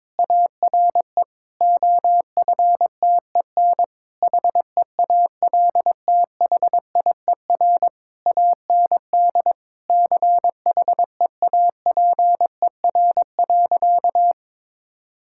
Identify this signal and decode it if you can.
Morse code, heard as a keyed tone, about 22 wpm, 700 Hz. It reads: ARE OFTEN HEALTHIER AND CHEAPER.
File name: Difficult Sentences with Hints - 22wpm